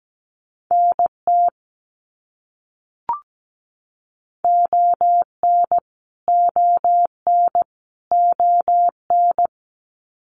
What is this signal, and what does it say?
Morse code, heard as a keyed tone, about 17 wpm, 700 Hz. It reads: NT ON ON ON
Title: Canadian Provinces and Territories - Repeated 3x - 17wpm